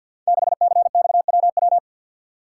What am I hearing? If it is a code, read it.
Morse code, heard as a keyed tone, about 48 wpm, 700 Hz. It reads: 6CXWW
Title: Code Groups - 5 Letter:Numbers - 50wpm